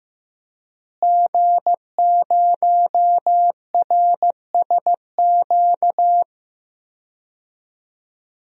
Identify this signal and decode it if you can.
Morse code, heard as a keyed tone, about 16 wpm, 700 Hz. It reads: G0RSQ
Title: DXCC-100 Call Signs - 5 Characters - Repeated 3x - 15wpm